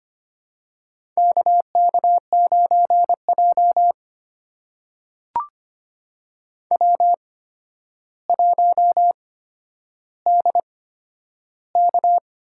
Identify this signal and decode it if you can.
Morse code, heard as a keyed tone, about 25 wpm, spaced as for 11 wpm, 700 Hz. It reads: KK9J W1DK
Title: USA Call Signs - 4 Characters - 8x Character Spacing - 25wpm